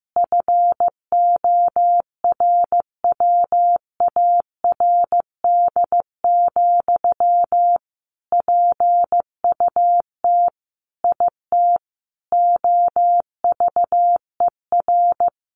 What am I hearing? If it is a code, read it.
Morse code, heard as a keyed tone, about 15 wpm, 700 Hz. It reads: FORWARD, PUT IT OVER
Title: Sentences from Top 1500 Words 15wpm